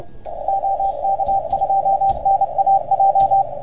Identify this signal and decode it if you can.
Morse code, heard as a keyed tone, about 29 wpm, 780 Hz, with radio noise. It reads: OK2NAJ
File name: Archiv / CUC381 / OK1FGD CUC 381